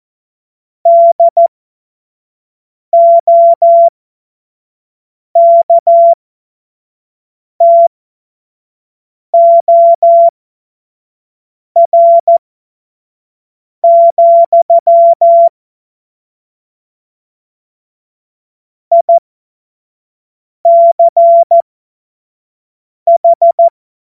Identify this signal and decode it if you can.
Morse code, heard as a keyed tone, about 14 wpm, spaced as for 5 wpm, 680 Hz. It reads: DOKTOR, ICH
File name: Morsekurs / Witze / Witz 23 April 2024 25 F70